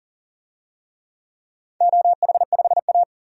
Morse code, heard as a keyed tone, about 40 wpm, 700 Hz. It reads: OH5A
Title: DXCC-100 Call Signs - 4 Characters - 40wpm